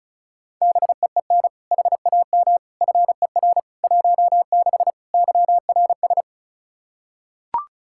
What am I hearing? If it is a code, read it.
Morse code, heard as a keyed tone, about 34 wpm, 700 Hz. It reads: BEEN HAM FER 16 YRS
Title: K1USN SST QSO Practice 35wpm